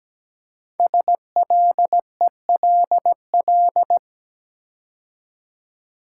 Morse code, heard as a keyed tone, about 17 wpm, 700 Hz. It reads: SLELL
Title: Code Groups - 5 Letters - 17wpm